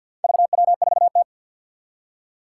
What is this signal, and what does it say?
Morse code, heard as a keyed tone, about 51 wpm, 700 Hz. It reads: VW4T
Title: Four Challenging Letter-Number Combinations 50wpm